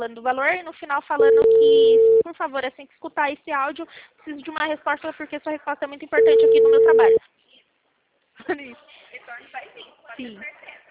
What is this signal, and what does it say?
Ringback tone (China, 450 Hz), with speech.